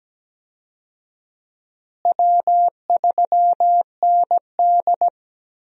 Morse code, heard as a keyed tone, about 17 wpm, 700 Hz. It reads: W3ND
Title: USA Call Signs - 4 Characters - Repeated 3x - 17wpm